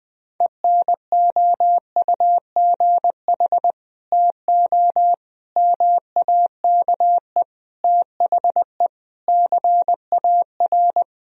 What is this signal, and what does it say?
Morse code, heard as a keyed tone, about 20 wpm, 700 Hz. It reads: ENOUGH TO MAKE THE CAR
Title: Sentences from Top 1000 Words - Review Part 1 - 20wpm